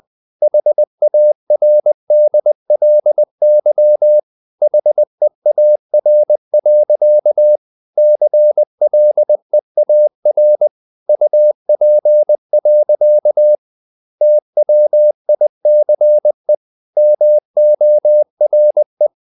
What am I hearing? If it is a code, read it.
Morse code, heard as a keyed tone, about 20 wpm, 600 Hz. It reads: HARDLY HEAR. CLEAR UP. TWICE MORE
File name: the old man and the sea part 2 20wpm